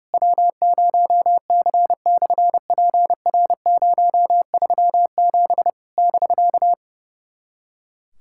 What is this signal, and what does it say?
Morse code, heard as a keyed tone, about 32 wpm, 700 Hz. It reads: W0C/PR037 <BK>
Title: SOTA QSO Practice 30wpm